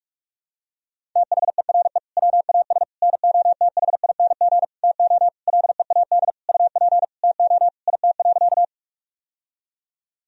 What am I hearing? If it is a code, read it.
Morse code, heard as a keyed tone, about 46 wpm, 700 Hz. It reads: THERE WAS NOTHING TO LEAD UP TO IT.
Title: Sentences from Top 1000 Words - Review Part 2 - 45wpm